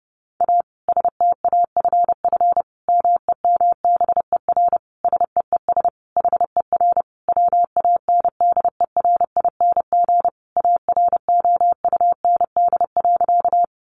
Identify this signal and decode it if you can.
Morse code, heard as a keyed tone, about 30 wpm, 700 Hz. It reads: A STAFF MEMBER SEES HER WANDERING AROUND.
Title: Additional Easy Phrases 30wpm